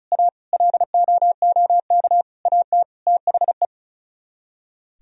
Morse code, heard as a keyed tone, about 46 wpm, 700 Hz. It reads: A LOOK AT THE